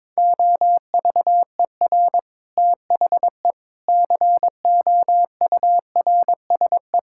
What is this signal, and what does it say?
Morse code, heard as a keyed tone, about 22 wpm, 700 Hz. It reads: OVER THE COURSE
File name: practice / 22wpm / Sentences with 4-Word Phrases - 22wpm